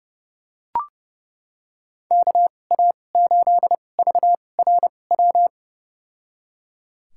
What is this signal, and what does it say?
Morse code, heard as a keyed tone, about 30 wpm, spaced as for 22 wpm, 700 Hz. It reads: KA8VRW
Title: USA Call Signs - 2x Character Spacing - 30wpm